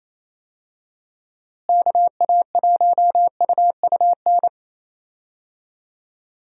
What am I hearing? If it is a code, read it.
Morse code, heard as a keyed tone, about 27 wpm, 700 Hz. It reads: KA1UUN
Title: USA Call Signs - 6 Characters - Repeated 2x - 28wpm